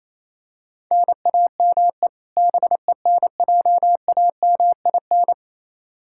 Morse code, heard as a keyed tone, about 27 wpm, 700 Hz. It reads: NAME BENJAMIN
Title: K1USN SST QSO Practice 28wpm